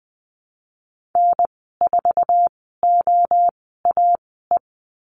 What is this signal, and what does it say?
Morse code, heard as a keyed tone, about 20 wpm, spaced as for 15 wpm, 700 Hz. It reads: N4OAE
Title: 048 Lesson 9g - Number 4 - Call Signs 20wpm